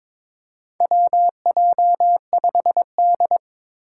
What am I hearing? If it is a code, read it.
Morse code, heard as a keyed tone, about 22 wpm, 700 Hz. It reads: WJ5D
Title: USA Call Signs - 4 Characters - 3x Character Spacing - 22wpm